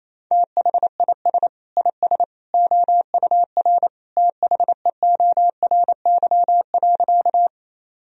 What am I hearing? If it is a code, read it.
Morse code, heard as a keyed tone, about 28 wpm, 700 Hz. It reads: THIS IS OUR THEORY.
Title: Sentences from Top 1000 Words - Review Part 3 - 28wpm